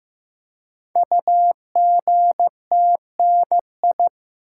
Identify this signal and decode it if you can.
Morse code, heard as a keyed tone, about 15 wpm, 700 Hz. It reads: UGTNI
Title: Code Groups - 5 Letters - 15wpm